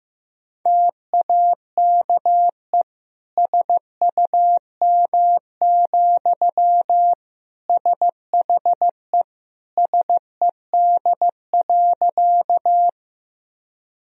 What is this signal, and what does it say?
Morse code, heard as a keyed tone, about 16 wpm, 700 Hz. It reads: TAKE SUM, SHE SED.
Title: Sentences from Top 200 Words and Abbreviations 15wpm